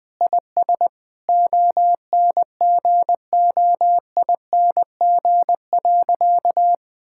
Morse code, heard as a keyed tone, about 20 wpm, 700 Hz. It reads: IS ONGOING.
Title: Sentences with 3-Word Phrases - 20wpm